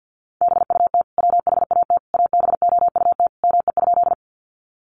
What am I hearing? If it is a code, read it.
Morse code, heard as a keyed tone, about 50 wpm, 700 Hz. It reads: BUT WHAT ABOUT ME?
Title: Sentences from Top 200 Words 50wpm